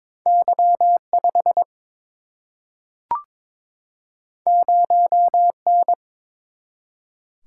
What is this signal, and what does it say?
Morse code, heard as a keyed tone, about 21 wpm, 700 Hz. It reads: Y 5 0 N
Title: Single-Letter Single-Number 22wpm